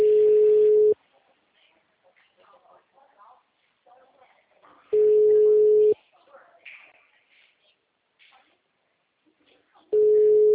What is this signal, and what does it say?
Ringback tone (Europe, 425 Hz), with speech.
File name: out- -2009- - - .56086